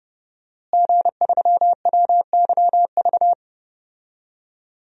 Morse code, heard as a keyed tone, about 30 wpm, 700 Hz. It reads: G3WYV
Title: DXCC-100 Call Signs - 5 Characters - Repeated 3x - 30wpm